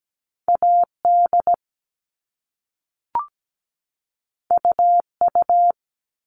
Morse code, heard as a keyed tone, about 17 wpm, 700 Hz. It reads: AD UU
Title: All Cut Numbers - 2-Digit 17wpm